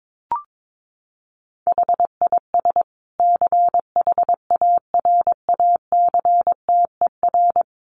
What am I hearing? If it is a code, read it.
Morse code, heard as a keyed tone, about 22 wpm, 700 Hz. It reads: HIS CHARACTER
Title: Sentences from Top 1300 Words 22wpm